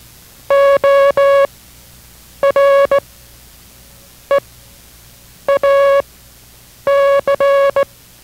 Morse code, heard as a keyed tone, about 17 wpm, spaced as for 6 wpm, 560 Hz. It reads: ORE A C